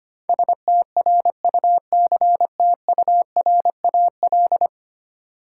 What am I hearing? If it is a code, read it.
Morse code, heard as a keyed tone, about 25 wpm, 700 Hz. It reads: STRUCTURAL
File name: Ten Letter Words - 25wpm